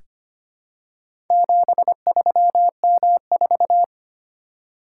Morse code, heard as a keyed tone, about 25 wpm, 700 Hz. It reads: 73M4
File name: practice / 25wpm / Four Letter-Number Combinations - 25wpm